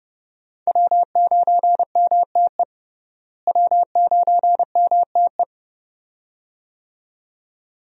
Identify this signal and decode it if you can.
Morse code, heard as a keyed tone, about 30 wpm, 700 Hz. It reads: W9MTE W9MTE
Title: USA Call Signs - 5 Characters - Repeated 2x - 30wpm